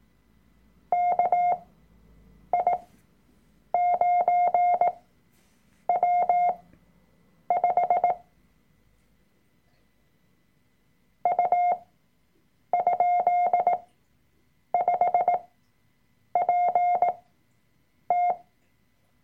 Morse code, heard as a keyed tone, about 18 wpm, spaced as for 7 wpm, 710 Hz. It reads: KI9W5 U?5PT
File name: Workshop Dateien / CW-Online-Kurs / Lektion 003